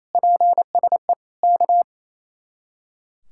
Morse code, heard as a keyed tone, about 34 wpm, 700 Hz. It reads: PSE K
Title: Speed-Racing Common QSO Phrases - 28wpm